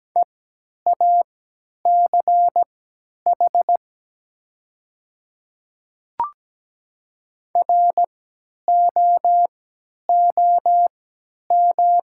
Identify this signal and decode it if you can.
Morse code, heard as a keyed tone, about 17 wpm, spaced as for 10 wpm, 690 Hz. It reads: EACH ROOM
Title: 3x Character Spacing - Top 300 Words - 17wpm